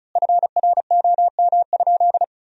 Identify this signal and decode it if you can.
Morse code, heard as a keyed tone, about 34 wpm, 700 Hz. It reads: FROM?
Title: 4x Word-Spacing - Sentences from Top 200 Words - 35wpm